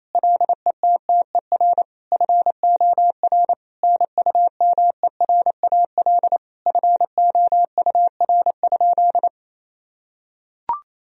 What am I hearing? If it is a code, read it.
Morse code, heard as a keyed tone, about 27 wpm, 700 Hz. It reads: LETTER FOR NUMERAL FOUR?
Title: Sentences from Top 1000 Words - Review Part 2 - 28wpm